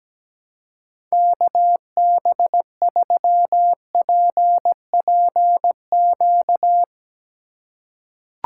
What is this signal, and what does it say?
Morse code, heard as a keyed tone, about 17 wpm, 700 Hz. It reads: KB3PPQ
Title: USA Call Signs - 6 Characters - 3x Character Spacing - 17wpm